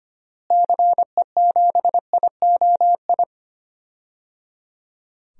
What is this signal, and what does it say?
Morse code, heard as a keyed tone, about 25 wpm, 700 Hz. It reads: CE7IOI
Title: DXCC-100 Call Signs - 6 Characters - Repeated 3x - 25wpm